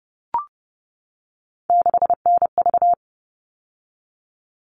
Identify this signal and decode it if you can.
Morse code, heard as a keyed tone, about 30 wpm, 700 Hz. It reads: 6NV
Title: All Cut Numbers - 3-Digit 30wpm